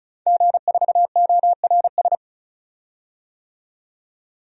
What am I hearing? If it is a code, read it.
Morse code, heard as a keyed tone, about 34 wpm, 700 Hz. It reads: G4ORS
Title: DXCC-100 Call Signs - 5 Characters - 35wpm